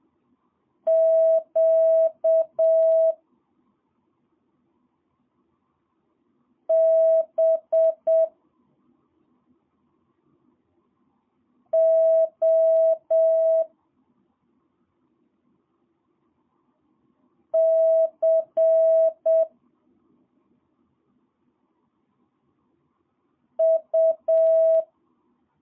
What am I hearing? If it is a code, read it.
Morse code, heard as a keyed tone, about 7 wpm, 650 Hz. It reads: Q B O C U
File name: jh8cbh / 2022cw